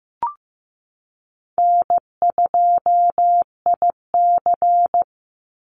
Morse code, heard as a keyed tone, about 15 wpm, 700 Hz. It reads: N2IC
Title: Easy Call Signs Repeated 3x - 15wpm